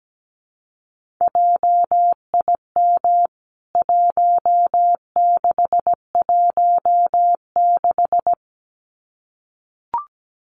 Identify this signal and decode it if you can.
Morse code, heard as a keyed tone, about 17 wpm, 700 Hz. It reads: JIM 1616
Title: CWT Exchange 17wpm